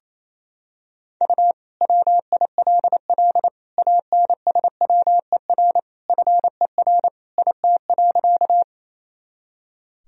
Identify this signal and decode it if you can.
Morse code, heard as a keyed tone, about 28 wpm, 700 Hz. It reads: U WILL ANSWER FER IT.
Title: Sentences from Top 300 Words and Abbreviations 28wpm